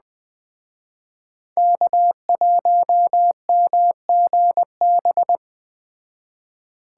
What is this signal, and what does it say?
Morse code, heard as a keyed tone, about 21 wpm, 700 Hz. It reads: K1MGB